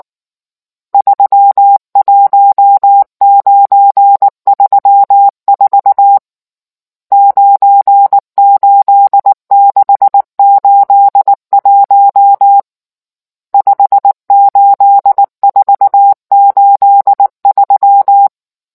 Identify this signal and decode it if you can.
Morse code, heard as a keyed tone, about 19 wpm, 800 Hz. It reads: 31934 98681 58483